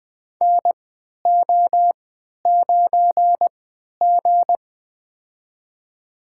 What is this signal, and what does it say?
Morse code, heard as a keyed tone, about 20 wpm, spaced as for 11 wpm, 700 Hz. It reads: NO9G
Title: USA Call Signs - 4 Characters - 3x Character Spacing - 20wpm